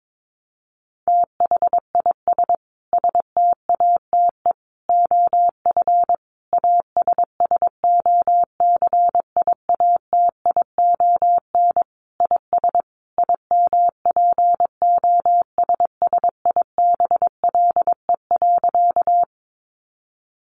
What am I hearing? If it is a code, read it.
Morse code, heard as a keyed tone, about 22 wpm, 700 Hz. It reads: THIS STATE OF ASSOCIATION IS IMPOSSIBLE.
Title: Sentences from Top 1100 Words 22wpm